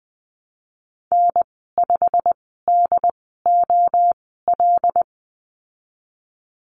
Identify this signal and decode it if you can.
Morse code, heard as a keyed tone, about 20 wpm, spaced as for 15 wpm, 700 Hz. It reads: N5DOL
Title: 128 Lesson 21g - Number 6 - Call Signs 20wpm